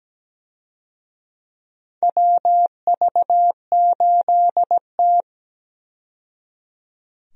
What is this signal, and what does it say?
Morse code, heard as a keyed tone, about 17 wpm, 700 Hz. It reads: WV8T
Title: USA Call Signs - 4 Characters - 17wpm